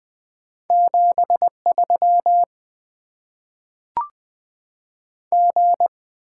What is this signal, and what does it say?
Morse code, heard as a keyed tone, about 20 wpm, 700 Hz. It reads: 7 3 G
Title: Top 20 QSO Elements - 8x Character Spacing - 20wpm